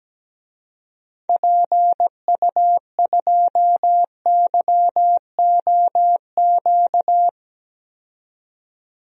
Morse code, heard as a keyed tone, about 17 wpm, 700 Hz. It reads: PU2YOQ
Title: DXCC-100 Call Signs - 6 Characters - 17wpm